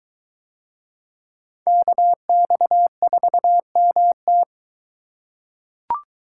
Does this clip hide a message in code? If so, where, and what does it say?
Yes: Morse code from 1.7 to 4.4 s, heard as a keyed tone, about 22 wpm, 700 Hz. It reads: KX4MT